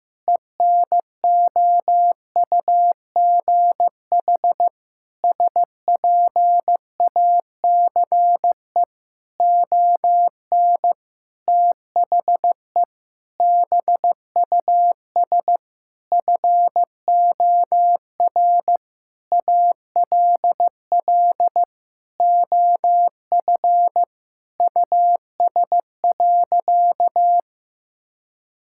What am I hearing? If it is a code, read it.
Morse code, heard as a keyed tone, about 15 wpm, 700 Hz. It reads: ENOUGH SPACE ON THE BUS FOR ALL OF US.
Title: Sentences from Top 1000 Words - Review Part 6 - 15wpm